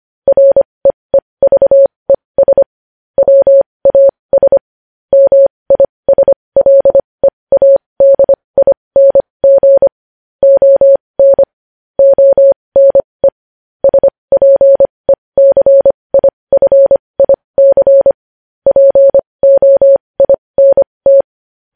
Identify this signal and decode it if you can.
Morse code, heard as a keyed tone, about 25 wpm, spaced as for 20 wpm, 550 Hz. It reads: REEVES WAS MISLEADING ON ONE SPECIFIC POINT
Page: News Headlines in Morse Code at 20 WPM